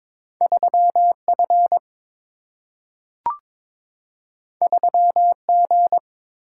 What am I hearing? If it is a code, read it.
Morse code, heard as a keyed tone, about 22 wpm, 700 Hz. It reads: 3 F 3 G
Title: Single-Letter Single-Number 22wpm